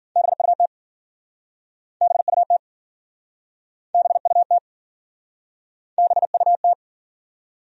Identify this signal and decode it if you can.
Morse code, heard as a keyed tone, about 52 wpm, 700 Hz. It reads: BUT BUT BUT BUT